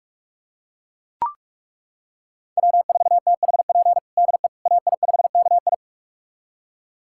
Morse code, heard as a keyed tone, about 41 wpm, 700 Hz. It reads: W4THP DE AI5KI
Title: USA Callsign DE Callsign - 5 Characters - 45wpm